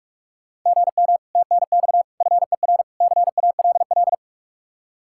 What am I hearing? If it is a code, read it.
Morse code, heard as a keyed tone, about 47 wpm, 700 Hz. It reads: GM TNX FER CALL